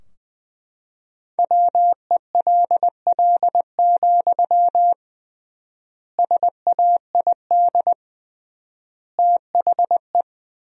Morse code, heard as a keyed tone, about 21 wpm, 700 Hz. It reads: WELL, SAID THE
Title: 3x Word-Spacing - Sentences from Top 400 Words - 20wpm